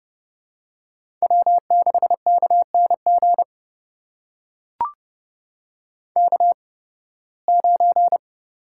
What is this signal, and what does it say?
Morse code, heard as a keyed tone, about 30 wpm, spaced as for 20 wpm, 700 Hz. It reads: W6KNG K9
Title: USA Call Signs - 5 Characters - 8x Character Spacing - 30wpm